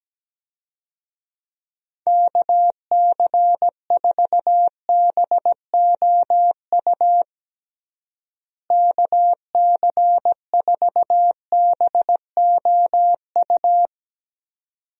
Morse code, heard as a keyed tone, about 17 wpm, 700 Hz. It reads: KC4BOU KC4BOU
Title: USA Call Signs - 6 Characters - Repeated 2x - 17wpm